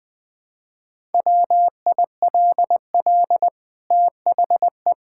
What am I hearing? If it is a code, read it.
Morse code, heard as a keyed tone, about 20 wpm, 700 Hz. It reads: WILL THE
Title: Sentences from Top 1000 Words - Review Part 5 - 20wpm